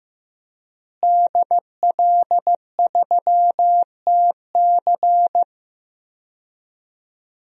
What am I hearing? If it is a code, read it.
Morse code, heard as a keyed tone, about 15 wpm, 700 Hz. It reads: DL3TC
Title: DXCC-100 Call Signs - 5 Characters - Repeated 3x - 15wpm